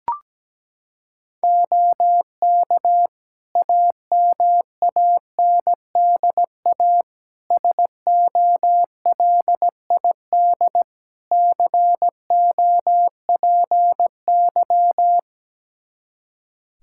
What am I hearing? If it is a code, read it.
Morse code, heard as a keyed tone, about 17 wpm, 700 Hz. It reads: OK AMANDA SOLID COPY
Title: K1USN SST QSO Practice 17wpm